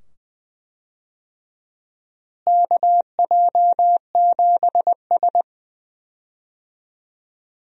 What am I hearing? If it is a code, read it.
Morse code, heard as a keyed tone, about 20 wpm, 700 Hz. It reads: KJ7S
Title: USA Call Signs - 4 Characters - Repeated 2x - 20wpm